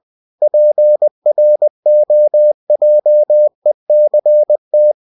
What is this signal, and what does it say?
Morse code, heard as a keyed tone, about 20 wpm, 600 Hz. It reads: PROJECT
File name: cw-books / winnie-the-pooh / winnie-20wpm-page-0113